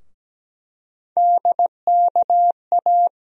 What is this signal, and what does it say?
Morse code, heard as a keyed tone, about 17 wpm, 700 Hz. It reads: DKA
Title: USA Call Sign Postfixes - 3 Letters 17wpm